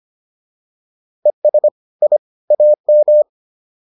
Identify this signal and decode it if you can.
Morse code, heard as a keyed tone, about 25 wpm, 600 Hz. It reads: ES I AM